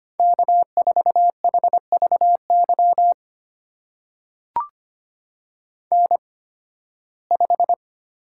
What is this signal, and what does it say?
Morse code, heard as a keyed tone, about 25 wpm, spaced as for 16 wpm, 700 Hz. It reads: K4HVY N5
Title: USA Call Signs - 5 Characters - 8x Character Spacing - 25wpm